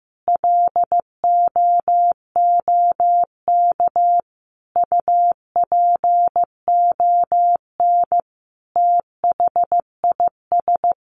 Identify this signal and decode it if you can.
Morse code, heard as a keyed tone, about 15 wpm, 700 Hz. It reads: LOOK UPON THIS
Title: Sentences from Top 1100 Words 15wpm